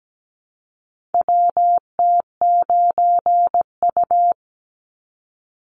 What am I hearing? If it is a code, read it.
Morse code, heard as a keyed tone, about 17 wpm, 700 Hz. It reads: WT9U
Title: Call Signs - 17wpm